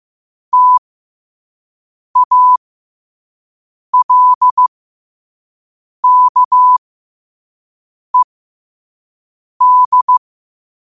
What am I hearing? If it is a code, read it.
Morse code, heard as a keyed tone, about 15 wpm, spaced as for 5 wpm, 1000 Hz. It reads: TALKED
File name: qotdfiles / 5wpm